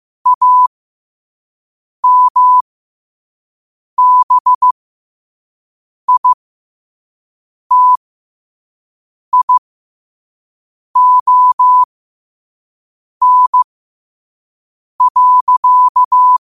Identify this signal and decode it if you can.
Morse code, heard as a keyed tone, about 15 wpm, spaced as for 5 wpm, 1000 Hz. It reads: AMBITION.
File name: qotdfiles / 5wpm